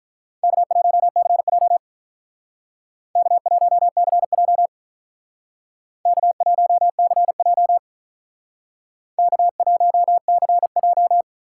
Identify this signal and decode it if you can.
Morse code, heard as a keyed tone, about 46 wpm, 700 Hz. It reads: K1CJ K1CJ K1CJ K1CJ